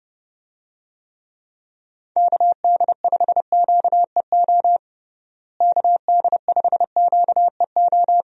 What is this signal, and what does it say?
Morse code, heard as a keyed tone, about 32 wpm, 700 Hz. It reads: KD5QEO KD5QEO
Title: USA Call Signs - 6 Characters - Repeated 3x - 30wpm